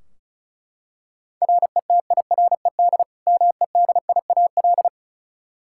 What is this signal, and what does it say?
Morse code, heard as a keyed tone, about 36 wpm, 700 Hz. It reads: RETIRED MEDIAL
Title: Practice QSO - Exchange 3 - More Info - 35wpm